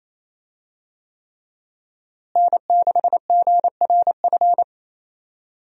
Morse code, heard as a keyed tone, about 28 wpm, 700 Hz. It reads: N6GRF
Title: USA Call Signs - 5 Characters - Repeated 3x - 28wpm